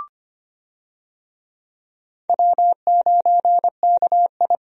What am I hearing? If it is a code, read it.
Morse code, heard as a keyed tone, about 25 wpm, 700 Hz. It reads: W9KI
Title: USA Call Signs - 4 Characters - Repeated 3x - 25wpm